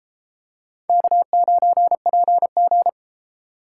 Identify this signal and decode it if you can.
Morse code, heard as a keyed tone, about 31 wpm, 700 Hz. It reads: K9PG